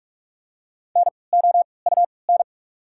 Morse code, heard as a keyed tone, about 43 wpm, spaced as for 25 wpm, 700 Hz. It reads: NOUN
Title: 3x Character Spacing - Top 500 Words - 45wpm